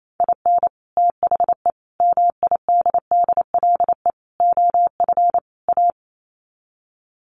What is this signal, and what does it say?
Morse code, heard as a keyed tone, about 28 wpm, 700 Hz. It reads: IN THE MIDDLE OF A
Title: Sets of 5 Words 28wpm